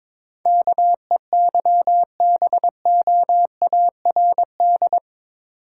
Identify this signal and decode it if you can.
Morse code, heard as a keyed tone, about 22 wpm, 700 Hz. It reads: KEYBOARD
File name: practice / 22wpm / Eight Letter Words - 22wpm